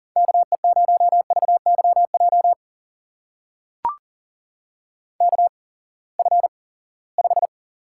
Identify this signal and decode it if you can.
Morse code, heard as a keyed tone, about 42 wpm, spaced as for 23 wpm, 700 Hz. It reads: KE0VYJ KF5
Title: USA Call Signs - 6 Characters - 8x Character Spacing - 40wpm